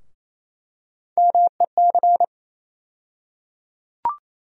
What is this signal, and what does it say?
Morse code, heard as a keyed tone, about 28 wpm, 700 Hz. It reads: MEC
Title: USA Call Sign Postfixes - 3 Letters 28wpm